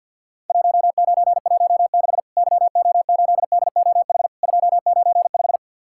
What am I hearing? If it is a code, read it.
Morse code, heard as a keyed tone, about 53 wpm, 700 Hz. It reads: 1916 YO8DOH 295
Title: Worked All Europe DX Contest - QTC Exchange - 50wpm